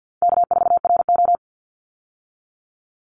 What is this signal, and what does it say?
Morse code, heard as a keyed tone, about 50 wpm, 700 Hz. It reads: K4RO
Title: Call Signs Repeated 3x - 50wpm